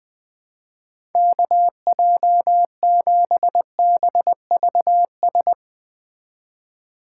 Morse code, heard as a keyed tone, about 20 wpm, 700 Hz. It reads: KJ7BVS
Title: USA Call Signs - 6 Characters - 3x Character Spacing - 20wpm